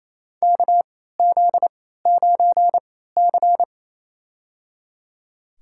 Morse code, heard as a keyed tone, about 28 wpm, spaced as for 16 wpm, 700 Hz. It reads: KZ9C
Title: USA Call Signs - 4 Characters - 3x Character Spacing - 28wpm